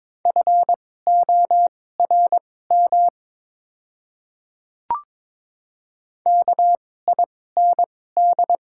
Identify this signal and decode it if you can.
Morse code, heard as a keyed tone, about 22 wpm, spaced as for 16 wpm, 700 Hz. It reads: FORM KIND
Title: 2x Character Spacing - Top 200 Words - 22wpm